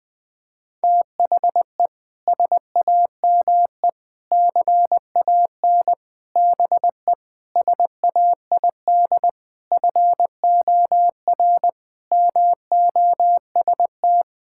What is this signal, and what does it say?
Morse code, heard as a keyed tone, about 20 wpm, 700 Hz. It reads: THE SAME CAN BE SAID FOR MOST
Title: Sentences with 6-Word Phrases - 20wpm